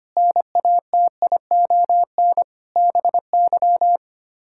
Morse code, heard as a keyed tone, about 25 wpm, 700 Hz. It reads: NATION BY
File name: Sets of 9 Words - 25wpm